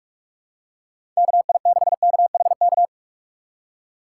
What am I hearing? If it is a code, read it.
Morse code, heard as a keyed tone, about 46 wpm, 700 Hz. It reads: KI6KHK
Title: USA Call Signs - 6 Characters - 3x Character Spacing - 45wpm